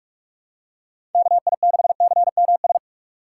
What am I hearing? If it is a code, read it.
Morse code, heard as a keyed tone, about 46 wpm, 700 Hz. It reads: KI6CMS
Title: USA Call Signs - 6 Characters - 2x Character Spacing - 45wpm